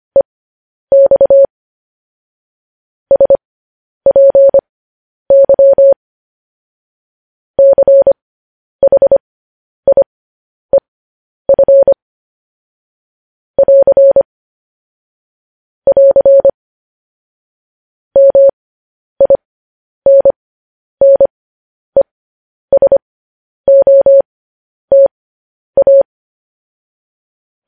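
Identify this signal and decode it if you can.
Morse code, heard as a keyed tone, about 25 wpm, spaced as for 10 wpm, 550 Hz. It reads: EX SPY CHIEF <AR> <AR> MINNESOTA
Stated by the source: Morse code news headlines.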